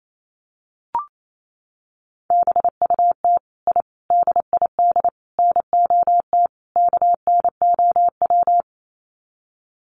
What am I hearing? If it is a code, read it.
Morse code, heard as a keyed tone, about 28 wpm, 700 Hz. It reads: BUT I DID NOT KNOW
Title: Sets of 5 Words Encore 28wpm